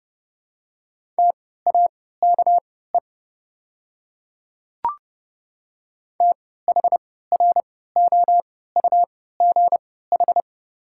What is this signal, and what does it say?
Morse code, heard as a keyed tone, about 30 wpm, spaced as for 17 wpm, 690 Hz. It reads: TAKE THROUGH
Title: 3x Character Spacing - Top 200 Words - 30wpm